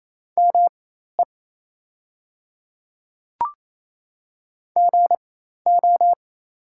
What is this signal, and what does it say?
Morse code, heard as a keyed tone, about 28 wpm, spaced as for 13 wpm, 700 Hz. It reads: ME GO
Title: 4x Character Spacing - Top 300 Words - 28wpm